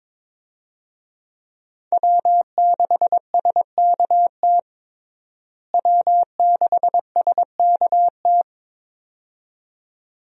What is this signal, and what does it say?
Morse code, heard as a keyed tone, about 22 wpm, 700 Hz. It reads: W6SKT W6SKT
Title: USA Call Signs - 5 Characters - Repeated 2x - 22wpm